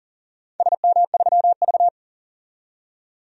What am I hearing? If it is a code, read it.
Morse code, heard as a keyed tone, about 40 wpm, 700 Hz. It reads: SM3V
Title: DXCC-100 Call Signs - 4 Characters - 40wpm